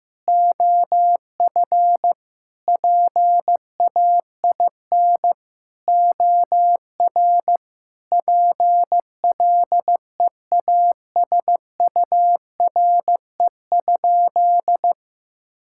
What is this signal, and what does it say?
Morse code, heard as a keyed tone, about 15 wpm, 700 Hz. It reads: OF PAIN OR PLEASURE?
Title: Sentences from Top 1600 Words 15wpm